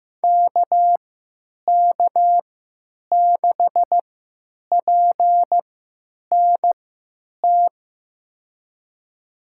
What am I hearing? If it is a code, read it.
Morse code, heard as a keyed tone, about 15 wpm, spaced as for 9 wpm, 700 Hz. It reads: KK6PNT
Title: USA Call Signs - 3x Character Spacing - 15wpm